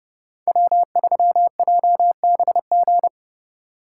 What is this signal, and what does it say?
Morse code, heard as a keyed tone, about 34 wpm, 700 Hz. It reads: W3JBG